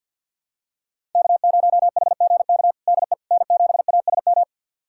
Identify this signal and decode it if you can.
Morse code, heard as a keyed tone, about 49 wpm, 700 Hz. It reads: K0HGK DE N7ASM
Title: USA Callsign DE Callsign - 5 Characters - 50wpm